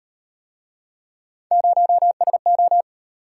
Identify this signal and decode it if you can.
Morse code, heard as a keyed tone, about 38 wpm, 700 Hz. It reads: 0SO